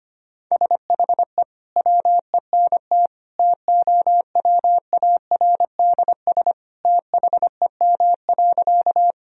Morse code, heard as a keyed tone, about 25 wpm, 700 Hz. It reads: SHE WENT TOWARDS THEM.
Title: Sentences from Top 1000 Words - Review Part 2 - 25wpm